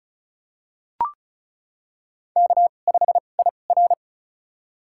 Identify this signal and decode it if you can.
Morse code, heard as a keyed tone, about 34 wpm, spaced as for 25 wpm, 700 Hz. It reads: K5IR
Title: USA Call Signs - 4 Characters - 2x Character Spacing - 35wpm